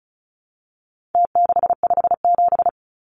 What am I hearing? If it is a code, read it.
Morse code, heard as a keyed tone, about 34 wpm, 700 Hz. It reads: T657
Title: Numbers - Contest Style - 4-Digit 35wpm